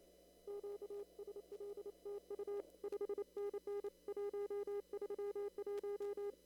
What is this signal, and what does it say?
Morse code, heard as a keyed tone, about 28 wpm, 410 Hz. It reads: QSL TU 5NN 131
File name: CW / kt5j wpx cw-op-k5pi